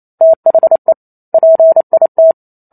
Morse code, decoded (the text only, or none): THE PIT